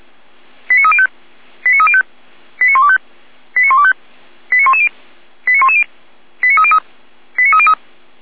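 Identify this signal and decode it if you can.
Five-tone selective call (ZVEI), heard as quick runs of five tones, tones of about 70 ms. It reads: 89305 89305 89125 89125 89110 89110 89303 89303
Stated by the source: ZVEI five-tone selective call received on SDR.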